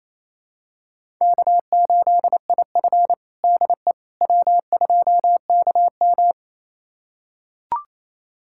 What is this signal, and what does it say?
Morse code, heard as a keyed tone, about 28 wpm, 700 Hz. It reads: K8IF DE W2KM
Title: USA Callsign DE Callsign - 4 Characters - 28wpm